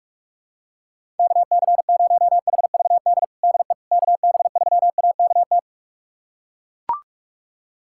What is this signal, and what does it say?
Morse code, heard as a keyed tone, about 42 wpm, 700 Hz. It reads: KC0HVD DE KB3AKT
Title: USA Callsign DE Callsign 45wpm